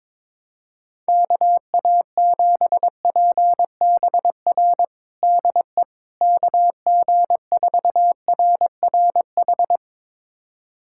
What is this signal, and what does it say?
Morse code, heard as a keyed tone, about 22 wpm, 700 Hz. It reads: KA7PBR DE KG4RRH
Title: USA Callsign DE Callsign - 6 Characters - 22wpm